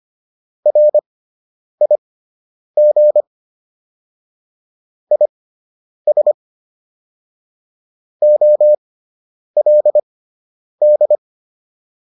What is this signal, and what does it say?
Morse code, heard as a keyed tone, about 25 wpm, spaced as for 9 wpm, 600 Hz. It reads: RIG IS OLD